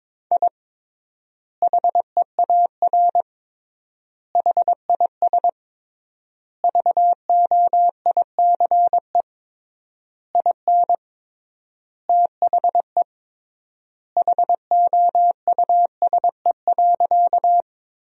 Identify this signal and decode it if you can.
Morse code, heard as a keyed tone, about 22 wpm, 700 Hz. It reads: I HEAR HIS VOICE IN THE HOUSE.
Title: 3x Word-Spacing - Sentences from Top 500 Words - 22wpm